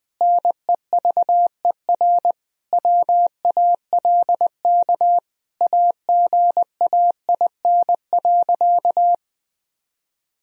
Morse code, heard as a keyed tone, about 21 wpm, 700 Hz. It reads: NEVER WALK AGAIN.
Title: Sentences from Top 1000 Words - Review Part 1 - 20wpm